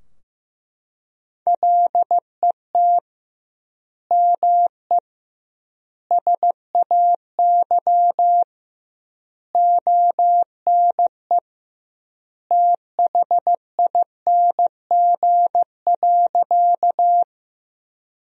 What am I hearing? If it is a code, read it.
Morse code, heard as a keyed tone, about 15 wpm, 700 Hz. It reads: LET ME SAY ONE THING.
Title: 2x Word-Spacing - Sentences from Top 300 Words - 15wpm